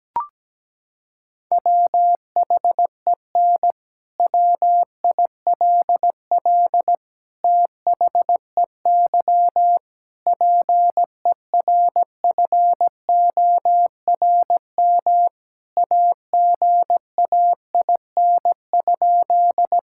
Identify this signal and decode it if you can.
Morse code, heard as a keyed tone, about 17 wpm, 700 Hz. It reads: WHEN WILL THEY PERFORM AGAIN?
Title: Sentences from Top 1000 Words - Review Part 5 - 17wpm